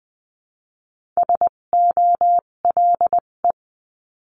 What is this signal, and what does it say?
Morse code, heard as a keyed tone, about 20 wpm, spaced as for 17 wpm, 700 Hz. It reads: SOLE
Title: 257 Lesson 42b - Words 14-20wpm